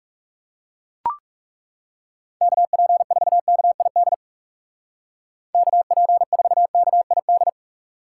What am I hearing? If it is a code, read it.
Morse code, heard as a keyed tone, about 38 wpm, 700 Hz. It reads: KP4KID KP4KID